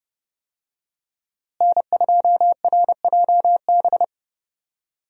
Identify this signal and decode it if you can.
Morse code, heard as a keyed tone, about 30 wpm, 700 Hz. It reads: N2RJB
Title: USA Call Signs - 5 Characters - Repeated 2x - 30wpm